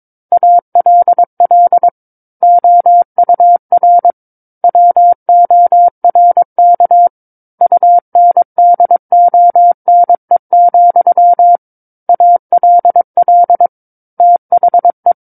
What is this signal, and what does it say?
Morse code, heard as a keyed tone, about 22 wpm, 700 Hz. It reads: ALL OUR WORK UNDONE, ALL THE
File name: Book One / 13-Chapter 13 - 22 WPM